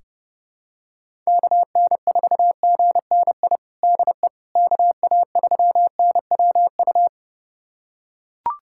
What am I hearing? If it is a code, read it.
Morse code, heard as a keyed tone, about 32 wpm, 700 Hz. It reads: KN4GNI DE KA3NWU
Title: USA Callsign DE Callsign - 6 Characters - 30wpm